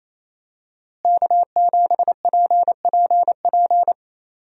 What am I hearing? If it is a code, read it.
Morse code, heard as a keyed tone, about 28 wpm, 700 Hz. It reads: K7PPP
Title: USA Call Signs - 5 Characters - Repeated 2x - 28wpm